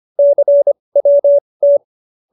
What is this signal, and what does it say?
Morse code, heard as a keyed tone, about 25 wpm, spaced as for 20 wpm, 570 Hz. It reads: CWT
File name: QSO316 20